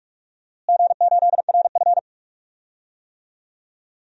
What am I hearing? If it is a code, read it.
Morse code, heard as a keyed tone, about 46 wpm, 700 Hz. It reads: G8RF
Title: DXCC-100 Call Signs - 4 Characters - Repeated 3x - 45wpm